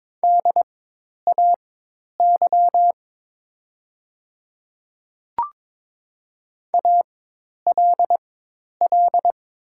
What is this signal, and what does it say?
Morse code, heard as a keyed tone, about 22 wpm, spaced as for 10 wpm, 700 Hz. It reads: DAY ALL
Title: 4x Character Spacing - Top 200 Words - 22wpm